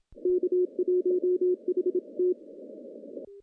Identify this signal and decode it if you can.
Morse code, heard as a keyed tone, about 25 wpm, 360 Hz. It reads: K1HT
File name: contests / CW-SS-k5tr- -0100